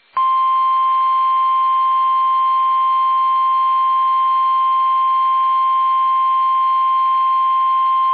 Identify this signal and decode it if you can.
Emergency Alert System (SAME) tones (the 1050 Hz attention tone). It is heard from 0.4 to 7.2 s.